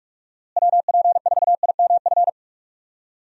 Morse code, heard as a keyed tone, about 42 wpm, 700 Hz. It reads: WP4IMF
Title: USA Call Signs - 6 Characters - Repeated 2x - 45wpm